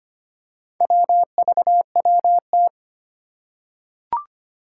Morse code, heard as a keyed tone, about 25 wpm, 700 Hz. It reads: WVWT
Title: Four Challenging Letter-Number Combinations 25wpm